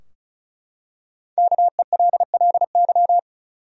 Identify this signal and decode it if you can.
Morse code, heard as a keyed tone, about 36 wpm, 700 Hz. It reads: KELLY
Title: USA Common Names - Max 5 Letters - 35wpm